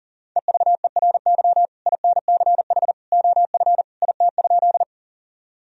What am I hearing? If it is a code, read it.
Morse code, heard as a keyed tone, about 40 wpm, 700 Hz. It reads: EVERY INCH OF IT?
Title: Sentences from Top 1000 Words - Review Part 3 - 40wpm